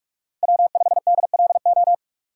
Morse code, heard as a keyed tone, about 46 wpm, 700 Hz. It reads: W5DLO
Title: USA Call Signs - 3x Character Spacing - 45wpm